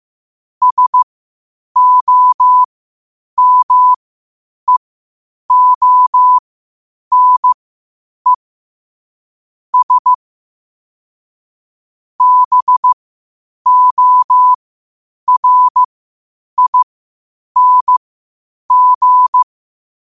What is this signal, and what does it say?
Morse code, heard as a keyed tone, about 15 wpm, spaced as for 8 wpm, 1000 Hz. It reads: SOMEONE S BORING
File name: qotdfiles / 8wpm